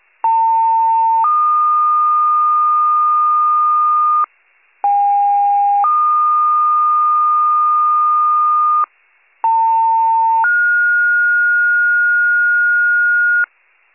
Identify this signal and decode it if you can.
Two-tone page, heard as a one-second tone, then a three-second tone, about 900 then 1220 Hz, about 810 then 1220 Hz, about 900 then 1430 Hz. It is heard from 0.2 to 13.4 s.